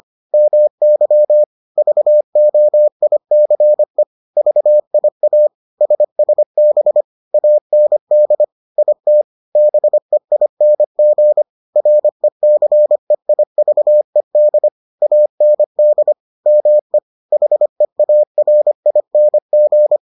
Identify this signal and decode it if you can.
Morse code, heard as a keyed tone, about 25 wpm, 600 Hz. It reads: MY VOICE VIA SSB AND IT BEING RECEIVED AND ME HEARING